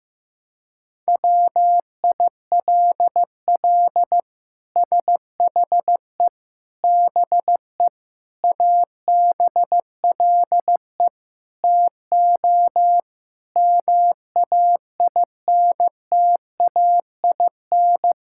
Morse code, heard as a keyed tone, about 16 wpm, 700 Hz. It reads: WILL SHE BE ABLE TO MAINTAIN
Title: Sentences from Top 1000 Words - Review Part 5 - 15wpm